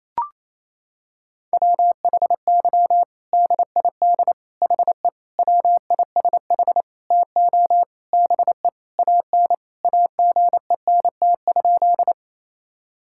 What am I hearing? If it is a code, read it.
Morse code, heard as a keyed tone, about 27 wpm, 700 Hz. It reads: WHY DID HE WISH TO BE AN AGENT?
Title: Sentences from Top 1600 Words 28wpm